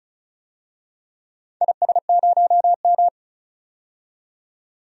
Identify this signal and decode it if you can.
Morse code, heard as a keyed tone, about 34 wpm, 700 Hz. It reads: IS0M
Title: DXCC-100 Call Signs - 4 Characters - 35wpm